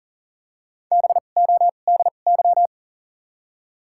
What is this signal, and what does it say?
Morse code, heard as a keyed tone, about 38 wpm, spaced as for 28 wpm, 700 Hz. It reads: BODY
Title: 2x Character Spacing - Top 500 Words - 40wpm